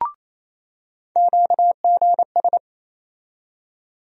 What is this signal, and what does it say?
Morse code, heard as a keyed tone, about 28 wpm, 700 Hz. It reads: QGS
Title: USA Call Sign Postfixes - 3 Letters 28wpm